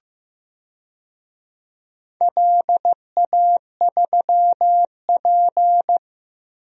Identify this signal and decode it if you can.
Morse code, heard as a keyed tone, about 15 wpm, 700 Hz. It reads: LA3P
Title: DXCC-100 Call Signs - 4 Characters - Repeated 3x - 15wpm